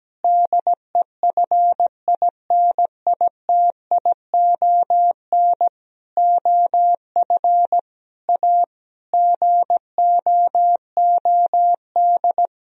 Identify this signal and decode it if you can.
Morse code, heard as a keyed tone, about 17 wpm, 700 Hz. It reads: DEFINITION OF A GOOD
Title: Sentences from Top 1000 Words - Review Part 4 - 17wpm